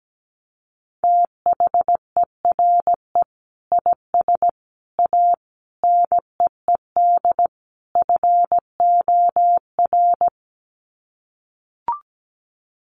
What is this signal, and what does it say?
Morse code, heard as a keyed tone, about 17 wpm, 700 Hz. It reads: THERE IS A NEED FOR
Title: Sets of 5 Words Encore2 17wpm